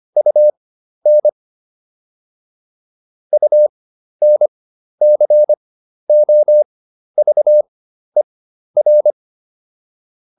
Morse code, heard as a keyed tone, about 25 wpm, spaced as for 12 wpm, 600 Hz. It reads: UN UNCOVER